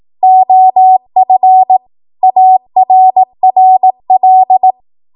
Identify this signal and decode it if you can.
Morse code, heard as a keyed tone, about 18 wpm, 750 Hz. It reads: OF ARRL
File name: Archive / 18 WPM / 18WPM